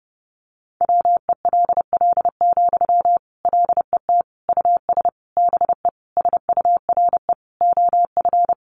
Morse code, heard as a keyed tone, about 30 wpm, 700 Hz. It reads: WELL, LET US BE SURE OF
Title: Sentences from Top 300 Words 30wpm